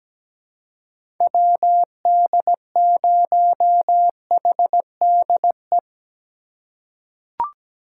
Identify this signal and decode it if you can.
Morse code, heard as a keyed tone, about 17 wpm, 700 Hz. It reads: WD0HDE
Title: USA Call Signs - 6 Characters - 4x Character Spacing - 17wpm